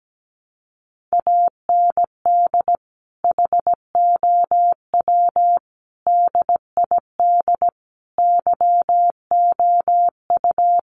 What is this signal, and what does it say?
Morse code, heard as a keyed tone, about 17 wpm, 700 Hz. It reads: AND HOW DID YOU
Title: Sentences from Top 100 Words 17wpm